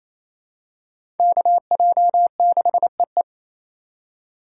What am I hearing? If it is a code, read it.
Morse code, heard as a keyed tone, about 27 wpm, 700 Hz. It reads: KJ6EE
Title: USA Call Signs - 2x Character Spacing - 28wpm